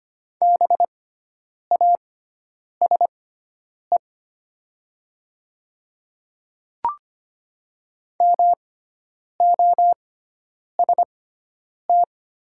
Morse code, heard as a keyed tone, about 25 wpm, spaced as for 9 wpm, 700 Hz. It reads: BASE MOST
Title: 6x Character Spacing - Top 300 Words - 25wpm